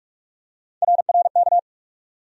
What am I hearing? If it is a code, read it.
Morse code, heard as a keyed tone, about 41 wpm, 700 Hz. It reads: RRK
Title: USA Call Sign Postfixes - 3 Letters 45wpm